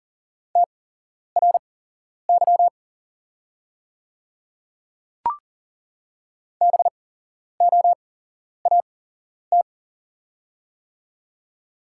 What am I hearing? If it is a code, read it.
Morse code, heard as a keyed tone, about 40 wpm, spaced as for 11 wpm, 700 Hz. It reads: TRY BOAT
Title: 8x Character Spacing - Top 500 Words - 40wpm